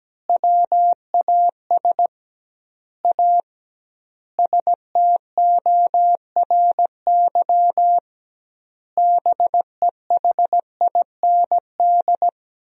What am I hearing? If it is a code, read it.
Morse code, heard as a keyed tone, about 17 wpm, 700 Hz. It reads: WAS A STORY BEHIND
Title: 2x Word-Spacing - Sentences from Top 500 Words - 17wpm